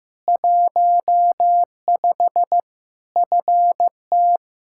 Morse code, heard as a keyed tone, about 15 wpm, 700 Hz. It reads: 15 FT
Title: Practice QSO - Exchange 2 - Info - 15wpm